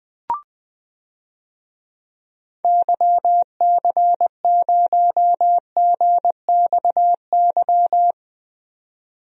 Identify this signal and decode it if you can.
Morse code, heard as a keyed tone, about 20 wpm, 700 Hz. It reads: YC0GXY
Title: DXCC-100 Call Signs - 6 Characters - 20wpm